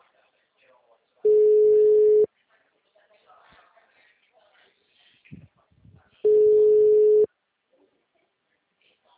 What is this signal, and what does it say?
Ringback tone (Europe, 425 Hz), with speech.
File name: out- -2013- - - .62254